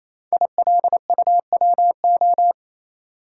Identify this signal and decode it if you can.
Morse code, heard as a keyed tone, about 29 wpm, 700 Hz. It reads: ILUWO
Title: Code Groups - 5 Letters - 28wpm